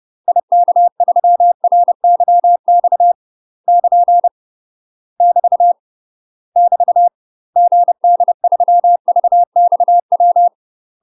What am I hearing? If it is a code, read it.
Morse code, heard as a keyed tone, about 30 wpm, 700 Hz. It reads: IK3RYX <KN> <BT> <BT> GD3VXW
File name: UK MASTER QSOs TRAINING / 30WPM / UK MASTER QSOs 0021